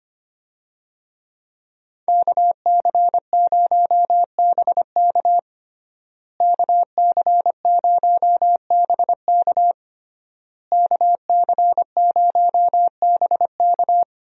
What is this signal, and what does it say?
Morse code, heard as a keyed tone, about 25 wpm, 700 Hz. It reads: KC0BK KC0BK KC0BK
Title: USA Call Signs - 5 Characters - Repeated 3x - 25wpm